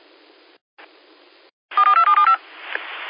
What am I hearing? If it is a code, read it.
DTMF, heard as telephone touch tones, tones of about 130 ms. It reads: 4275
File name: qsorec SimplexLogic